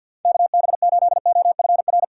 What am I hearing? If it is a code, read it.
Morse code, heard as a keyed tone, about 53 wpm, 700 Hz. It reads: KB8OFR
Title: USA Call Signs - 6x Character Spacing - 50wpm